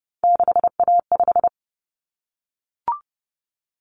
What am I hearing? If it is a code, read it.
Morse code, heard as a keyed tone, about 30 wpm, 700 Hz. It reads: 6A5
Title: Common Cut Numbers - 3-Digit 30wpm